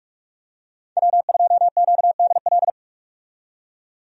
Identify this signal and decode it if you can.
Morse code, heard as a keyed tone, about 47 wpm, 700 Hz. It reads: W2QDL
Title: USA Call Signs - 5 Characters - 45wpm